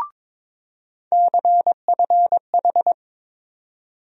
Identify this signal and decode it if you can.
Morse code, heard as a keyed tone, about 22 wpm, 700 Hz. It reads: CFH
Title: USA Call Sign Postfixes - 3 Letters 22wpm